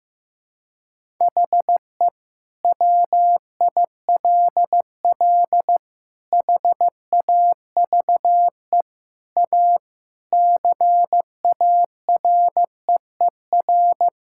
Morse code, heard as a keyed tone, about 15 wpm, 700 Hz. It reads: HE WILL HAVE A CAREER
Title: Sentences from Top 1000 Words - Review Part 6 - 15wpm